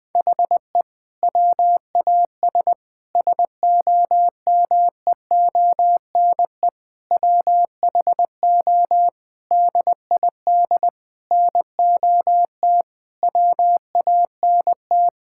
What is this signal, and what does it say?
Morse code, heard as a keyed tone, about 20 wpm, 700 Hz. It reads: HE WAS SOMEONE WHO DID NOT WANT
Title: Sentences with 6-Word Phrases - 20wpm